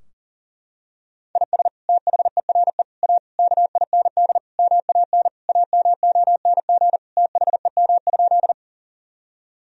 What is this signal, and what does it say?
Morse code, heard as a keyed tone, about 40 wpm, 700 Hz. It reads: IS THERE A KIND MAN AMONG THEM?
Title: Sentences from Top 1000 Words - Review Part 3 - 40wpm